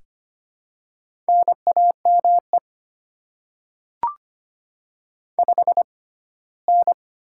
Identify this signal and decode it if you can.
Morse code, heard as a keyed tone, about 25 wpm, spaced as for 17 wpm, 700 Hz. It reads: NAME 5N
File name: Top 20 QSO Elements - 6x Character Spacing - 25wpm